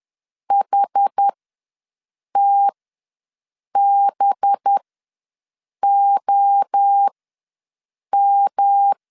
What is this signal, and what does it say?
Morse code, heard as a keyed tone, about 10 wpm, spaced as for 6 wpm, 800 Hz. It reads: HTBOM
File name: LEZIONI / 5 LBVC / 5 LB ws 10wpm 3616